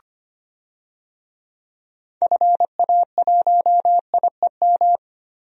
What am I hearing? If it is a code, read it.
Morse code, heard as a keyed tone, about 25 wpm, 700 Hz. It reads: FA1IEM
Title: DXCC-100 Call Signs - 6 Characters - 25wpm